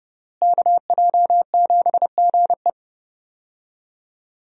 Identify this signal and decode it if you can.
Morse code, heard as a keyed tone, about 31 wpm, 700 Hz. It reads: KJ7GE
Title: USA Call Signs - 5 Characters - Repeated 3x - 30wpm